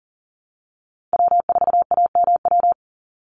Morse code, heard as a keyed tone, about 40 wpm, 700 Hz. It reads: W4AMW
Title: NA SOTA Chaser Callsigns 40wpm